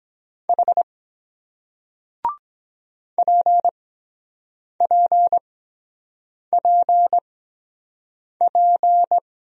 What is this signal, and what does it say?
Morse code, heard as a keyed tone, about 24 wpm, spaced as for 7 wpm, 700 Hz. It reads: H PPPP